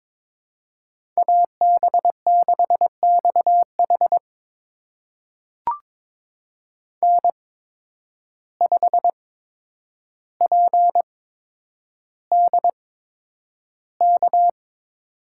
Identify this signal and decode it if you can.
Morse code, heard as a keyed tone, about 22 wpm, spaced as for 9 wpm, 700 Hz. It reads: AB6XH N5PDK
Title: USA Call Signs - 5 Characters - 8x Character Spacing - 22wpm